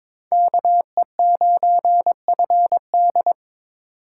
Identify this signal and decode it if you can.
Morse code, heard as a keyed tone, about 26 wpm, 700 Hz. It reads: KE9FD
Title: Speed Racing - USA 2x2 Call Signs 22wpm